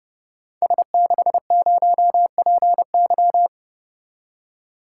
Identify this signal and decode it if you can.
Morse code, heard as a keyed tone, about 30 wpm, 700 Hz. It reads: S60PY
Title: Code Groups - 5 Letter:Numbers - 30wpm